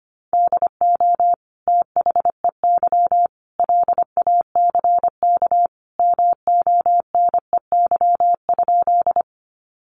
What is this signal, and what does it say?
Morse code, heard as a keyed tone, about 25 wpm, 700 Hz. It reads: DO THEY LACK MONEY?
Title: Sentences from Top 1100 Words 25wpm